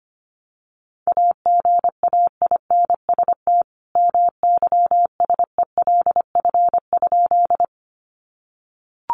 Morse code, heard as a keyed tone, about 25 wpm, 700 Hz. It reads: AGAINST MYSELF?
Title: Sentences from Top 1300 Words 25wpm